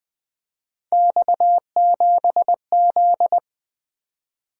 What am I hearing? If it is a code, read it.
Morse code, heard as a keyed tone, about 20 wpm, 700 Hz. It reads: X7Z
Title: Three Letter-Number Combinations - 20wpm